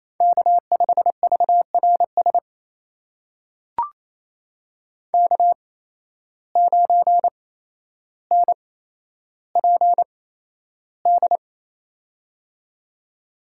Morse code, heard as a keyed tone, about 28 wpm, spaced as for 12 wpm, 700 Hz. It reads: K5VRS K9NPD
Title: USA Call Signs - 5 Characters - 8x Character Spacing - 28wpm